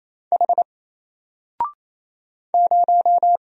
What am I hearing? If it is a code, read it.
Morse code, heard as a keyed tone, about 27 wpm, 700 Hz. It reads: H 0
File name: Single-Letter-Number-28wpm